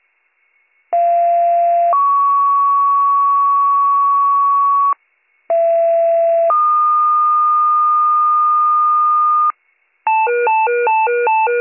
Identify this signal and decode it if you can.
Two-tone page, heard as a one-second tone, then a three-second tone, about 690 then 1120 Hz, about 670 then 1220 Hz. It is heard from 0.9 to 9.5 s.